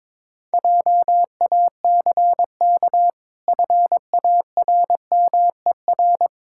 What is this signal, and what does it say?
Morse code, heard as a keyed tone, about 22 wpm, 700 Hz. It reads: JACK FARMER
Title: Pangram Challenge v2 - 22wpm